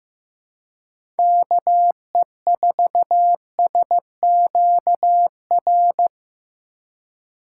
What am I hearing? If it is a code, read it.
Morse code, heard as a keyed tone, about 15 wpm, 700 Hz. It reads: KE4SQR
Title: USA Call Signs - 6 Characters - 3x Character Spacing - 15wpm